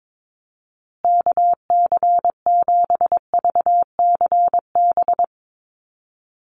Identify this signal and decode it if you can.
Morse code, heard as a keyed tone, about 22 wpm, 700 Hz. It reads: KC7VCB
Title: USA Call Signs Repeated 2x 22wpm